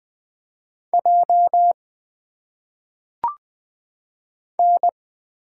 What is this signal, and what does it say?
Morse code, heard as a keyed tone, about 20 wpm, 700 Hz. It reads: J N
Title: Single Letters - Phonetic Pronunciation 20wpm